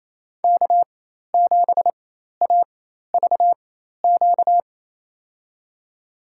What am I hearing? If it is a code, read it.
Morse code, heard as a keyed tone, about 28 wpm, spaced as for 13 wpm, 690 Hz. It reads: K7AVQ
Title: USA Call Signs - 5 Characters - 4x Character Spacing - 28wpm